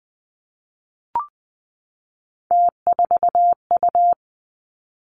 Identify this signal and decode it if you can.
Morse code, heard as a keyed tone, about 19 wpm, 700 Hz. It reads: T4U
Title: Three Challenging Letter-Number Combinations 20wpm